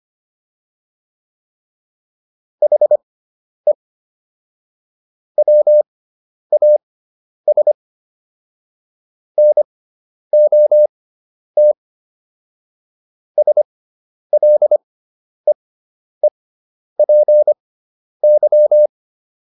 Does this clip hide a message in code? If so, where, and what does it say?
Yes: Morse code from 2.6 to 18.9 s, heard as a keyed tone, about 25 wpm, spaced as for 10 wpm, 600 Hz. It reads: HE WAS NOT SLEEPY